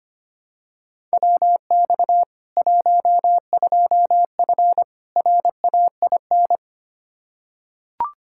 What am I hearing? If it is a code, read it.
Morse code, heard as a keyed tone, about 25 wpm, 700 Hz. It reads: WX 12F RAIN
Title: K1USN SST QSO Practice 25wpm